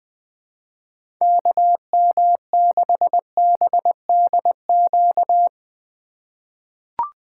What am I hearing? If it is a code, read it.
Morse code, heard as a keyed tone, about 20 wpm, 700 Hz. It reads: KM6BDQ
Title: USA Call Signs - 4x Character Spacing - 20wpm